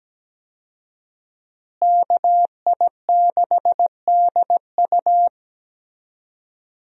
Morse code, heard as a keyed tone, about 17 wpm, 700 Hz. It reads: KI6DU
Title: USA Call Signs - 5 Characters - Repeated 2x - 17wpm